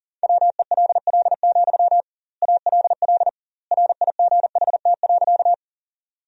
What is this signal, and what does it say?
Morse code, heard as a keyed tone, about 40 wpm, 700 Hz. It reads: WELL, ALL RIGHT.
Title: 2x Word-Spacing - Sentences from Top 200 Words - 40wpm